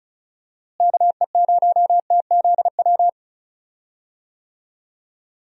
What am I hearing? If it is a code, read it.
Morse code, heard as a keyed tone, about 36 wpm, 700 Hz. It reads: KE0TZW
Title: USA Call Signs - 6 Characters - Repeated 3x - 35wpm